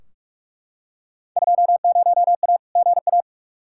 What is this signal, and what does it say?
Morse code, heard as a keyed tone, about 43 wpm, 700 Hz. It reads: 20A GA
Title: Field Day Exchanges Encore - 45wpm